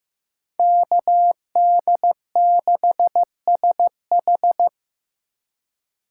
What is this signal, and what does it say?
Morse code, heard as a keyed tone, about 18 wpm, 700 Hz. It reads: KD6SH
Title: Speed Racing - USA 2x2 Call Signs 15wpm